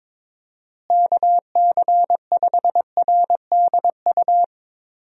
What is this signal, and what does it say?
Morse code, heard as a keyed tone, about 21 wpm, 700 Hz. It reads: KC5RDU
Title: USA Call Signs - 6 Characters - Repeated 3x - 22wpm